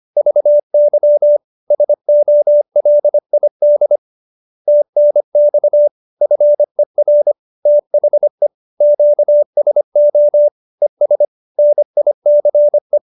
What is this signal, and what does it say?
Morse code, heard as a keyed tone, about 25 wpm, 590 Hz. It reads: VY SOLID TNX FER THE QSO ES NICE